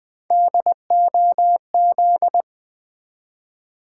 Morse code, heard as a keyed tone, about 20 wpm, 700 Hz. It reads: DOZ
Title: Build A Word - 1 Letter At A Time - 3 to 5-Letter Words - 20wpm